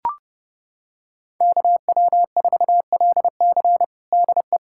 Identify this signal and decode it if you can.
Morse code, heard as a keyed tone, about 30 wpm, 700 Hz. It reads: KW4LC DE
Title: USA Callsign DE Callsign 30wpm